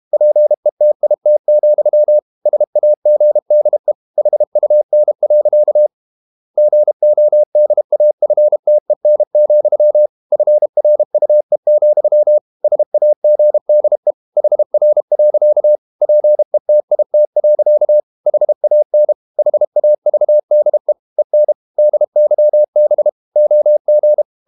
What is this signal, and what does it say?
Morse code, heard as a keyed tone, about 32 wpm, 600 Hz. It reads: PETIT, SAGDE HUN. GODAFTEN, FRUE, SAGDE HR. PETIT. HAN HAVDE EN DYB OG
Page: Ferie 32 wpm